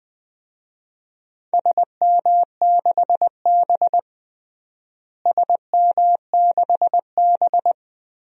Morse code, heard as a keyed tone, about 21 wpm, 700 Hz. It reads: SM6B SM6B
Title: DXCC-100 Call Signs - 4 Characters - Repeated 2x - 20wpm